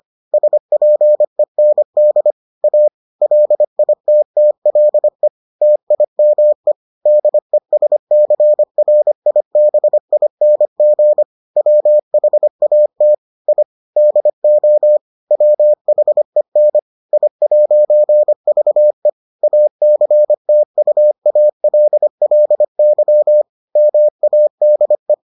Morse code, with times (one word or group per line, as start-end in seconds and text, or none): 0.3-2.3 SPEND
2.6-2.9 A
3.2-5.3 LITTLE
5.6-6.7 TIME
7.1-11.2 DESCRIBING
11.6-13.1 WHAT
13.5-13.6 I
14.0-15.0 DO
15.3-16.8 WHEN
17.1-19.1 I'VE
19.4-23.4 ACTUALLY
23.8-25.1 MADE